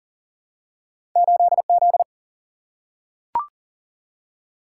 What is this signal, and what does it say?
Morse code, heard as a keyed tone, about 39 wpm, 700 Hz. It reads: 8 Z
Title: Single-Letter Single-Number 40wpm